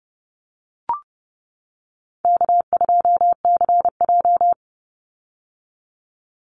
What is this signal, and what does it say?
Morse code, heard as a keyed tone, about 30 wpm, 700 Hz. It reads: K2CJ
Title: Call Signs - 30wpm